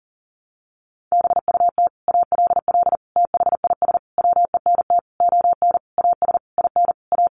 Morse code, heard as a keyed tone, about 40 wpm, 700 Hz. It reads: BUT ALL THIS WENT ON AS IN A
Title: Sentences from Top 1500 Words 40wpm